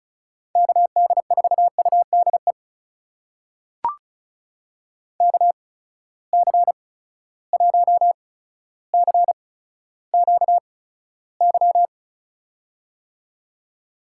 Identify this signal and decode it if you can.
Morse code, heard as a keyed tone, about 34 wpm, spaced as for 15 wpm, 700 Hz. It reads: KD4UDE KC1CQY
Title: USA Call Signs - 6 Characters - 8x Character Spacing - 35wpm